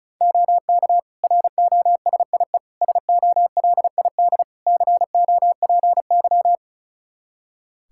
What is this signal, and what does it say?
Morse code, heard as a keyed tone, about 36 wpm, 700 Hz. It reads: OK ROSIE SOLID COPY
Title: Practice QSO - Exchange 3 - More Info - 35wpm